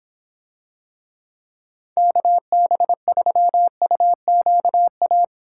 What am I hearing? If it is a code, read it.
Morse code, heard as a keyed tone, about 26 wpm, 700 Hz. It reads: KB3UQA